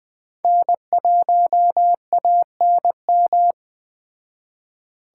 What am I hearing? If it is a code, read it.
Morse code, heard as a keyed tone, about 20 wpm, 700 Hz. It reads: N1ANM